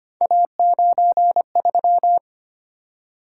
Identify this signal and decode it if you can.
Morse code, heard as a keyed tone, about 25 wpm, 700 Hz. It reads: A93
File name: Three Letter-Number Combinations - 25wpm